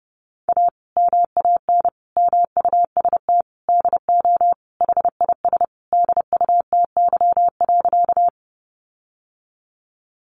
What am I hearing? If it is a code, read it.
Morse code, heard as a keyed tone, about 30 wpm, 700 Hz. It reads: A MAN MUST DO HIS DUTY.
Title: Sentences from Top 1500 Words 30wpm